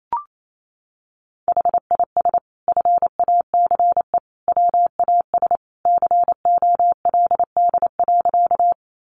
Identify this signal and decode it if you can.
Morse code, heard as a keyed tone, about 28 wpm, 700 Hz. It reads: HIS FACE WAS COLD.
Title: A Sound Of Thunder 28wpm